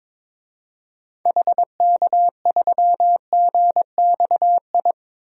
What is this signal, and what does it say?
Morse code, heard as a keyed tone, about 22 wpm, 700 Hz. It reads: HK3GXI
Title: DXCC-100 Call Signs - 6 Characters - 22wpm